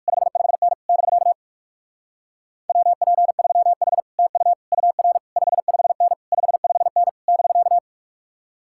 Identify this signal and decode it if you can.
Morse code, heard as a keyed tone, about 61 wpm, spaced as for 53 wpm, 700 Hz. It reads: 55N <BK> WP3H TU UR 55N 55N <BK>
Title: Speed-Racing Common QSO Phrases Encore - 45wpm